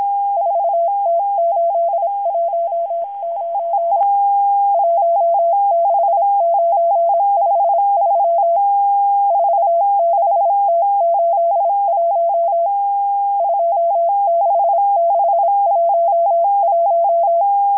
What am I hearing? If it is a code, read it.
Morse code, heard as a keyed tone, about 26 wpm, 680 Hz. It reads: 4T819 16953 46T81 26611
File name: HM02- -0437utc-7351khz